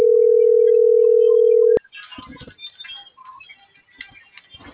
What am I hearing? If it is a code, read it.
Ringback tone (North America), 440 + 480 Hz.